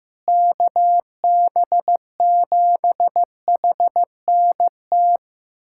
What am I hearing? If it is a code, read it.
Morse code, heard as a keyed tone, about 15 wpm, 700 Hz. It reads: KB7HNT
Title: USA Callsign DE Callsign - 6 Characters - 15wpm